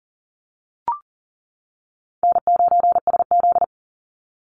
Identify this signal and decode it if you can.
Morse code, heard as a keyed tone, about 40 wpm, 700 Hz. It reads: N9SZ
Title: Easy Call Signs Repeated 2x - 40wpm